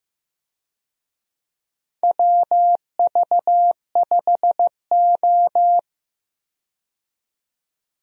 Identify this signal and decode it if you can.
Morse code, heard as a keyed tone, about 15 wpm, 700 Hz. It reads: WV5O
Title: USA Call Signs - 4 Characters - 15wpm